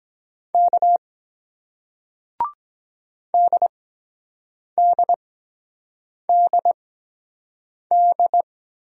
Morse code, heard as a keyed tone, about 23 wpm, spaced as for 7 wpm, 700 Hz. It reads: K DDDD